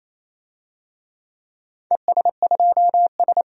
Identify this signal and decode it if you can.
Morse code, heard as a keyed tone, about 28 wpm, 700 Hz. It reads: ES2S
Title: DXCC-100 Call Signs - 4 Characters - 28wpm